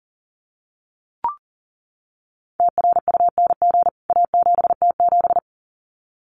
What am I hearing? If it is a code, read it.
Morse code, heard as a keyed tone, about 40 wpm, 700 Hz. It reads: TRUNG A7T7
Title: CWT Exchange 40wpm